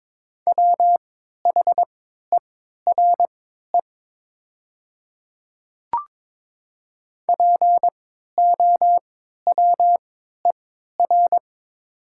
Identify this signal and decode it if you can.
Morse code, heard as a keyed tone, about 22 wpm, spaced as for 13 wpm, 700 Hz. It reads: WHERE POWER
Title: 3x Character Spacing - Top 500 Words - 22wpm